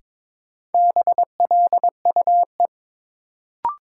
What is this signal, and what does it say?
Morse code, heard as a keyed tone, about 22 wpm, 700 Hz. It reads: BLUE
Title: Sentences from Top 1200 Words - New Word Drill Alt - 22wpm